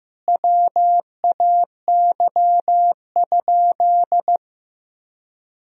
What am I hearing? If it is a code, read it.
Morse code, heard as a keyed tone, about 16 wpm, 700 Hz. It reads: WAY?
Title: Sentences from Top 1000 Words - Review Part 1 - 15wpm